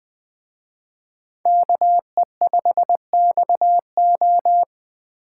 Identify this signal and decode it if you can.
Morse code, heard as a keyed tone, about 20 wpm, 700 Hz. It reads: KE5XO
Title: Speed Racing - USA 2x2 Call Signs 17wpm